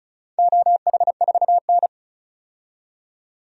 Morse code, heard as a keyed tone, about 34 wpm, 700 Hz. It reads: OH4N
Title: DXCC-100 Call Signs - 4 Characters - 35wpm